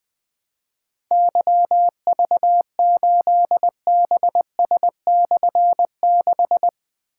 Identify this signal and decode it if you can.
Morse code, heard as a keyed tone, about 20 wpm, 700 Hz. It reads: YV8BS/6
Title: Difficult Call Signs 20wpm